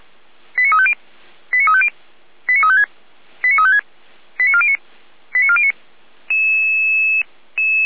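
Five-tone selective call (ZVEI), heard as quick runs of five tones, tones of about 70 ms. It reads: 89350 89350 89356 89356 89409 89409
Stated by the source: ZVEI five-tone selective call received via SDR, from the Feuerwehr Orlamünde fire brigade audio archive.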